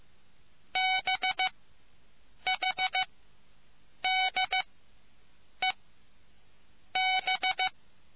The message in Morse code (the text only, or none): BHDEB